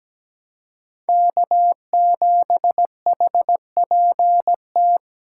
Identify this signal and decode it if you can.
Morse code, heard as a keyed tone, about 17 wpm, 700 Hz. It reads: K7HPT
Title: K1USN SST QSO Practice - 17wpm